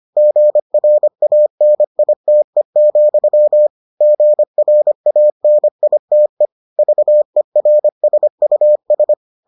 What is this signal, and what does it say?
Morse code, heard as a keyed tone, about 25 wpm, 600 Hz. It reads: GRANITE, GRANITE VERSUS